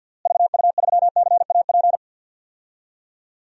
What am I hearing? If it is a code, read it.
Morse code, heard as a keyed tone, about 50 wpm, 700 Hz. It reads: VU3CAP
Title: DXCC-100 Call Signs - 6 Characters - Repeated 3x - 50wpm